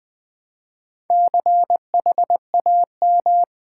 Morse code, heard as a keyed tone, about 20 wpm, 700 Hz. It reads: CHAM
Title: Build A Word - 2 Letters At A Time - 8-Letter Words - 20wpm